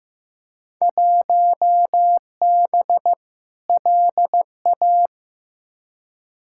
Morse code, heard as a keyed tone, about 15 wpm, 700 Hz. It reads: 1B LA
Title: Field Day Exchanges Encore - 15wpm